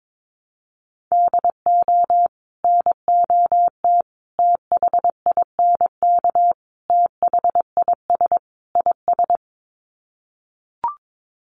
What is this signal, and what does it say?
Morse code, heard as a keyed tone, about 22 wpm, 700 Hz. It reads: DO NOT THINK THIS IS
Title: Sets of 5 Words Encore2 22wpm